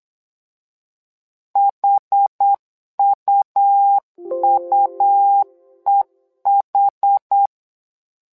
Morse code, heard as a keyed tone, about 8 wpm, 800 Hz. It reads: HUUEH
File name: LEZIONI / 2 UHDN / 2 UH 8wpm 2129